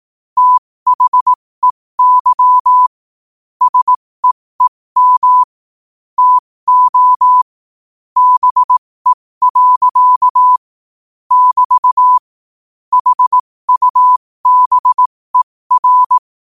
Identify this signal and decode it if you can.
Morse code, heard as a keyed tone, about 18 wpm, spaced as for 15 wpm, 1000 Hz. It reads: THEY SEEM TO BE. <BT> HUBER
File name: qotdfiles / 15wpm